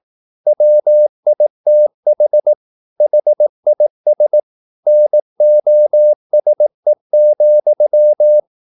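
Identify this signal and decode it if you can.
Morse code, heard as a keyed tone, about 18 wpm, 600 Hz. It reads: WITH HIS NOSE,
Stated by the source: Morse practice at 18 wpm, English text, from CW med Gnister.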